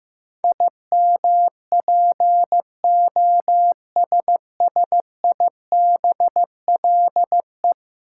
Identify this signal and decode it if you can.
Morse code, heard as a keyed tone, about 15 wpm, 700 Hz. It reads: IMPOSSIBLE
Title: Sentences from Top 1100 Words - New Word Drill - 15wpm